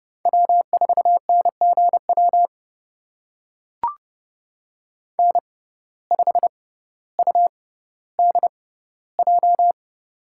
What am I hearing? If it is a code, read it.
Morse code, heard as a keyed tone, about 30 wpm, spaced as for 15 wpm, 700 Hz. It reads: W4NGW N5UDJ
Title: USA Call Signs - 5 Characters - 6x Character Spacing - 30wpm